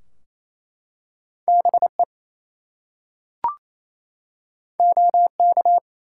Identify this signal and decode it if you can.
Morse code, heard as a keyed tone, about 27 wpm, 700 Hz. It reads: BE OK
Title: Two Letter Words - 28wpm